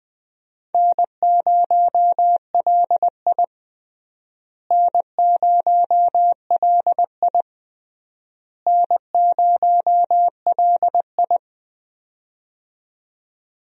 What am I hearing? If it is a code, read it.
Morse code, heard as a keyed tone, about 21 wpm, 700 Hz. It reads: N0LI N0LI N0LI
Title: USA Call Signs - 4 Characters - Repeated 3x - 20wpm